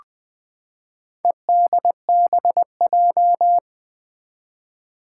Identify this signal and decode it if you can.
Morse code, heard as a keyed tone, about 20 wpm, 700 Hz. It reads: EDBJ
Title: Four Challenging Letter-Number Combinations 20wpm